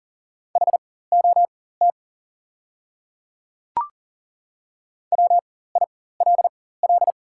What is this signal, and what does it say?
Morse code, heard as a keyed tone, about 37 wpm, spaced as for 18 wpm, 690 Hz. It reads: HOT WILL
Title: 4x Character Spacing - Top 100 Words - 40wpm